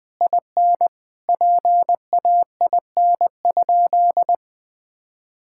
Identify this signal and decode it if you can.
Morse code, heard as a keyed tone, about 20 wpm, 700 Hz. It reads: IN PAIN?
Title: Sentences from Top 1600 Words 20wpm